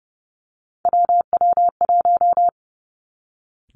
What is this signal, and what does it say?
Morse code, heard as a keyed tone, about 30 wpm, 700 Hz. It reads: WW1
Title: Three Challenging Letter-Number Combinations 30wpm